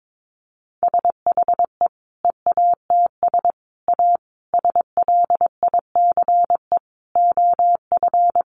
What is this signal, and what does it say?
Morse code, heard as a keyed tone, about 22 wpm, 700 Hz. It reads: SHE EATS A SLICE OF
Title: Additional Easy Phrases 22wpm